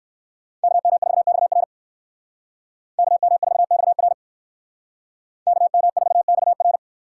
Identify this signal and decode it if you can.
Morse code, heard as a keyed tone, about 58 wpm, 700 Hz. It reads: KM4XR KM4XR KM4XR